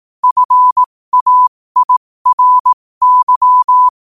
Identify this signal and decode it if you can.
Morse code, heard as a keyed tone, about 18 wpm, spaced as for 15 wpm, 1000 Hz. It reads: FAIRY
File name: qotdfiles / 15wpm